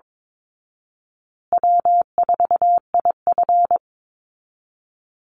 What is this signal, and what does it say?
Morse code, heard as a keyed tone, about 22 wpm, 700 Hz. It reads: W4IF
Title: Easy Call Signs Repeated 3x - 22wpm